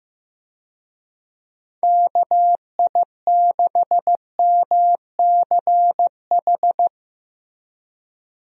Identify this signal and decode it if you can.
Morse code, heard as a keyed tone, about 15 wpm, 700 Hz. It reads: KI6MCH
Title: USA Call Signs - 6 Characters - Repeated 2x - 15wpm